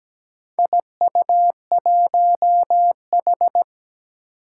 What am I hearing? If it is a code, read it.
Morse code, heard as a keyed tone, about 17 wpm, 700 Hz. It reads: IU1H
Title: Four Challenging Letter-Number Combinations 17wpm